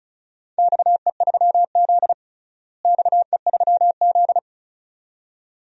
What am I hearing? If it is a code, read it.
Morse code, heard as a keyed tone, about 36 wpm, 700 Hz. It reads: XE3Z XE3Z
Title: DXCC-100 Call Signs - 4 Characters - Repeated 2x - 35wpm